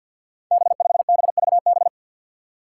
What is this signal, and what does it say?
Morse code, heard as a keyed tone, about 53 wpm, 700 Hz. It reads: B5BVB
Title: Five Challenging Letter-Number Combinations 50wpm